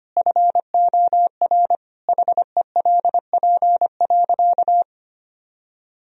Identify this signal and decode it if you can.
Morse code, heard as a keyed tone, about 25 wpm, 700 Hz. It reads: FOR HELP.
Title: Sentences from Top 1000 Words - Review Part 2 - 25wpm